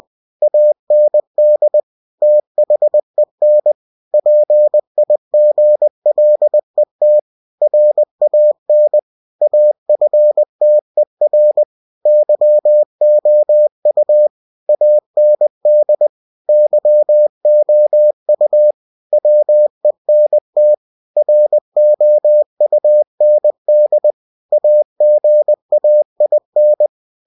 Morse code, heard as a keyed tone, about 20 wpm, 600 Hz. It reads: AND THEN PIGLET RAN AFTER YOU AND YOU WENT ROUND AGAIN
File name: cw-books / winnie-the-pooh / winnie-20wpm-page-0025